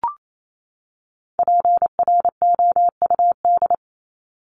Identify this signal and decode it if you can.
Morse code, heard as a keyed tone, about 28 wpm, 700 Hz. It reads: PROUD
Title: Top 1501-1750 Words 28wpm